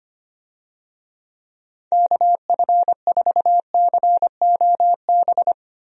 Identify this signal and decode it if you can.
Morse code, heard as a keyed tone, about 25 wpm, 700 Hz. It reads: KF4COB
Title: USA Call Signs - 6 Characters - 25wpm